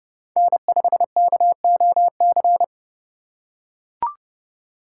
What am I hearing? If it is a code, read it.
Morse code, heard as a keyed tone, about 28 wpm, 700 Hz. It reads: N5KOC